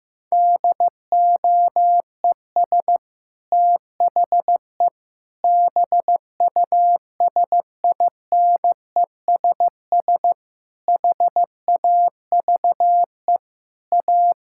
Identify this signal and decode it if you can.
Morse code, heard as a keyed tone, about 15 wpm, 700 Hz. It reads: DOES THE BUSINESS HAVE A
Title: Sentences from Top 1600 Words 15wpm